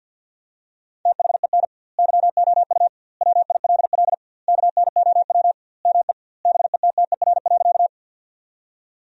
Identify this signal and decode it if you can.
Morse code, heard as a keyed tone, about 48 wpm, 700 Hz. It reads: THEN YOU WILL KNOW ME BETTER.
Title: 2x Word-Spacing - Sentences from Top 400 Words - 50wpm